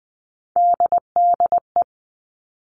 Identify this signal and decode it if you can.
Morse code, heard as a keyed tone, about 20 wpm, 700 Hz. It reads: DDE
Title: All Cut Numbers - 3-Digit 20wpm